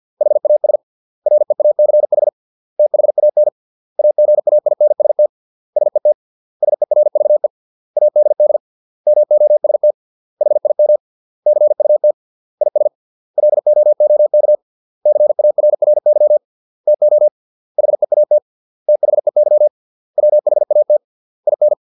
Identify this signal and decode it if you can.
Morse code, heard as a keyed tone, about 50 wpm, 600 Hz. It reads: HAS REACH THAN AGAINST SET SERVE ADD MOST HIM CUT IS LOOK CARRY TO HEAT THEY WHAT IN